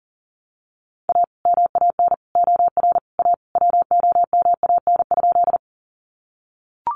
Morse code, heard as a keyed tone, about 40 wpm, 700 Hz. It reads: A MAN OR A WOMAN?
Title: Sentences from Top 1100 Words 40wpm